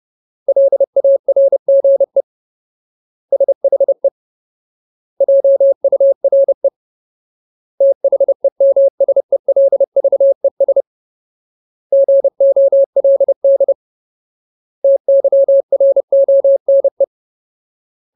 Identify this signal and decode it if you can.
Morse code, heard as a keyed tone, about 30 wpm, 550 Hz. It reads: LARGE SHE JURE THEMSELVES GOLD TYRONE
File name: Ultimate Word List / 30WPM Ultimate